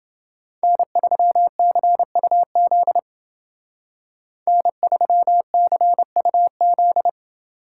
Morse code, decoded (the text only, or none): N3CUZ N3CUZ